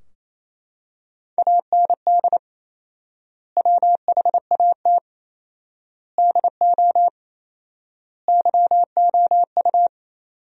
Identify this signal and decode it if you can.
Morse code, heard as a keyed tone, about 27 wpm, 700 Hz. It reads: AND WHAT DO YOU
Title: 4x Word-Spacing - Sentences from Top 200 Words - 28wpm